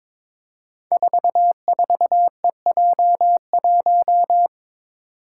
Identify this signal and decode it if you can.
Morse code, heard as a keyed tone, about 22 wpm, 700 Hz. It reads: 44EJ1
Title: Code Groups - 5 Letter:Numbers - 22wpm